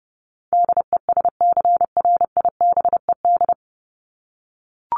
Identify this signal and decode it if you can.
Morse code, heard as a keyed tone, about 30 wpm, 700 Hz. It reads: DESCRIBED
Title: ED Suffix - Top 100 Words with Base Words 30wpm